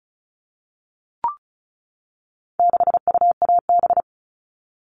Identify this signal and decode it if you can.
Morse code, heard as a keyed tone, about 34 wpm, 700 Hz. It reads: 6UAB
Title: All Cut Numbers - 4-Digit 35wpm